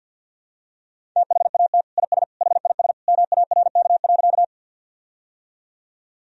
Morse code, heard as a keyed tone, about 46 wpm, 700 Hz. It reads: THAT IS HIS MARK.
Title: Sentences from Top 1000 Words - Review Part 1 - 50wpm